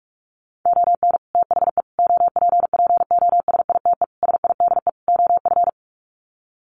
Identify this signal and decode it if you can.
Morse code, heard as a keyed tone, about 44 wpm, 700 Hz. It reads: ON THE OPPOSITE SIDE OF
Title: Sets of 5 Words Encore2 45wpm